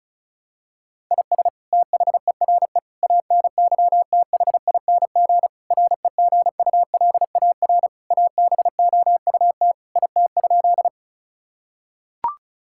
Morse code, heard as a keyed tone, about 36 wpm, 700 Hz. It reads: IS THERE ANYTHING REGULAR ABOUT IT?
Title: Sentences from Top 1000 Words - Review Part 6 - 35wpm